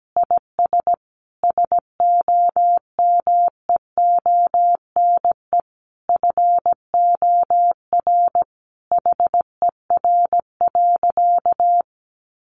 Morse code, heard as a keyed tone, about 17 wpm, 700 Hz. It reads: IS SOMEONE FOR HER.
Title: Sentences from Top 600 Words 17wpm